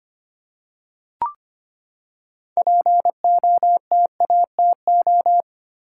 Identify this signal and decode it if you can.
Morse code, heard as a keyed tone, about 25 wpm, 700 Hz. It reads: POTATO
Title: Six Letter Words - 25wpm